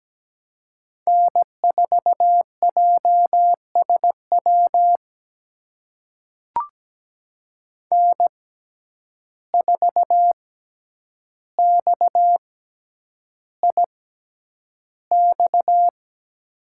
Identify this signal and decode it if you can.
Morse code, heard as a keyed tone, about 17 wpm, spaced as for 9 wpm, 700 Hz. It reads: N4JSW N4XIX
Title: USA Call Signs - 5 Characters - 6x Character Spacing - 17wpm